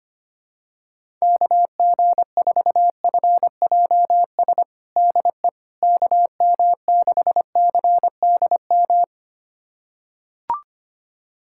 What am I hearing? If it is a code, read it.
Morse code, heard as a keyed tone, about 25 wpm, 700 Hz. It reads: KG4FJS DE KM6CDM
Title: USA Callsign DE Callsign 25wpm